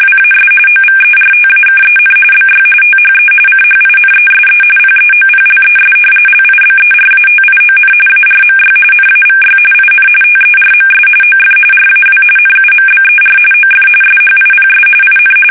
RTTY, about 45.45 baud. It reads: WAS OVERTAKEN BY THE CHINESE PEOPLES LIBERATION ARMY NAVY IN TERMS OF RAW NUMBER OF SHIPS IN
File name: fsk850 / usnavy1